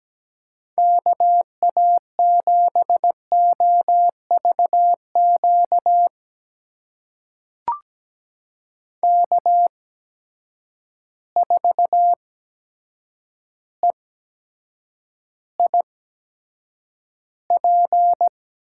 Morse code, heard as a keyed tone, about 17 wpm, spaced as for 7 wpm, 700 Hz. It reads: KA7OV Q K4EIP
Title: USA Call Signs - 8x Character Spacing - 17wpm